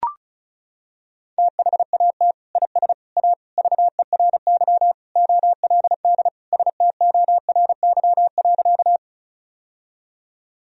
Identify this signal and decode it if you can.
Morse code, heard as a keyed tone, about 36 wpm, 700 Hz. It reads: THAT IS A VERY OLD STORY.
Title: Sentences from Top 1000 Words - Review Part 1 - 35wpm